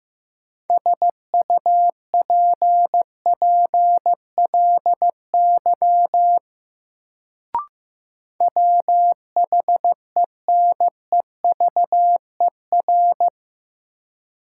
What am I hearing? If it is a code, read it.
Morse code, heard as a keyed tone, about 15 wpm, 700 Hz. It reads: SUPPLY WHENEVER
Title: Sentences from Top 1200 Words - New Word Drill Alt - 15wpm